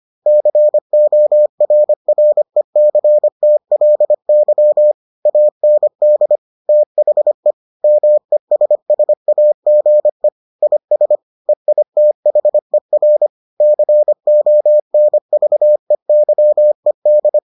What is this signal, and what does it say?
Morse code, heard as a keyed tone, about 25 wpm, 600 Hz. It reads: CORRECTLY AND THE MESSAGE IS EITHER CONVEYED